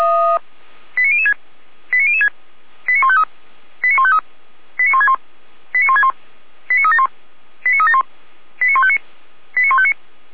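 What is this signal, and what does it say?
Five-tone selective call (ZVEI), heard as quick runs of five tones, tones of about 70 ms. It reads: 89006 89006 89153 89153 89161 89161 89371 89371 89159 89159
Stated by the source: ZVEI five-tone selective call, address 89165, received by SDR.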